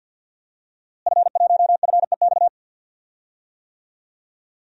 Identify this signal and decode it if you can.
Morse code, heard as a keyed tone, about 53 wpm, 700 Hz. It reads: F1FEX
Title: DXCC-100 Call Signs - 5 Characters - Repeated 2x - 50wpm